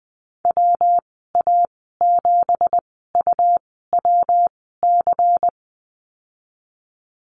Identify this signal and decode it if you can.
Morse code, heard as a keyed tone, about 20 wpm, spaced as for 15 wpm, 700 Hz. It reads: WA7UWC
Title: 167 Lesson 27g - Number 7 - Call Signs 20wpm